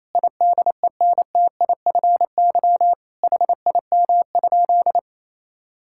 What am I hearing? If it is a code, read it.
Morse code, heard as a keyed tone, about 28 wpm, 700 Hz. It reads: IDENTIFY HIM?
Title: Sentences from Top 1000 Words - Review Part 6 - 28wpm